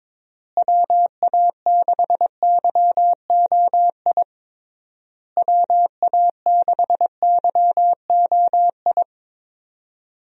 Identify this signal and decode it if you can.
Morse code, heard as a keyed tone, about 22 wpm, 700 Hz. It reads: WA6YOI WA6YOI
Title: USA Call Signs - 6 Characters - Repeated 3x - 22wpm